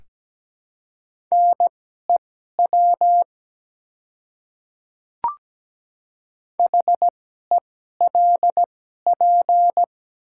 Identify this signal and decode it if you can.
Morse code, heard as a keyed tone, about 17 wpm, spaced as for 12 wpm, 700 Hz. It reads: NEW HELP
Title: 2x Character Spacing - Top 300 Words - 17wpm